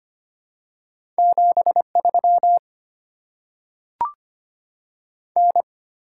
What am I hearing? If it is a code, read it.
Morse code, heard as a keyed tone, about 25 wpm, 700 Hz. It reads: 73 N
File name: Top 20 QSO Elements - 6x Character Spacing - 25wpm